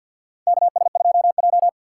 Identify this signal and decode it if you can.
Morse code, heard as a keyed tone, about 47 wpm, 700 Hz. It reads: KS2J
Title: USA Call Signs - 4 Characters - 6x Character Spacing - 50wpm